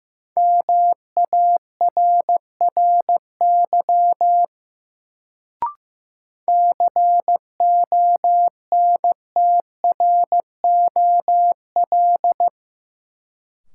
Morse code, heard as a keyed tone, about 15 wpm, 700 Hz. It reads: MARRY CONTROL
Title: Sentences from Top 1200 Words - New Word Drill Alt - 15wpm